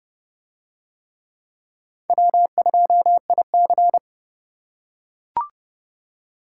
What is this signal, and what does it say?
Morse code, heard as a keyed tone, about 29 wpm, 700 Hz. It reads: W2IC